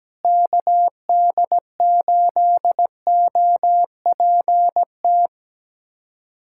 Morse code, heard as a keyed tone, about 17 wpm, 700 Hz. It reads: KD8OPT
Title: USA Callsign DE Callsign 17wpm